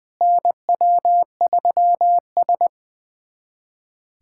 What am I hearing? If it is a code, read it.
Morse code, heard as a keyed tone, about 20 wpm, 700 Hz. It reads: NW3S
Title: USA Callsign DE Callsign - 4 Characters - 20wpm